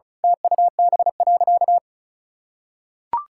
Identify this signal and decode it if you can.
Morse code, heard as a keyed tone, about 34 wpm, 700 Hz. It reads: TUB.
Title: Pangram Challenge v2 - 35wpm